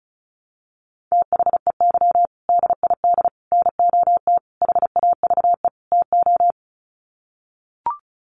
Morse code, heard as a keyed tone, about 34 wpm, 700 Hz. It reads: THEY DID NOT HAVE TO
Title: Sets of 5 Words Encore2 35wpm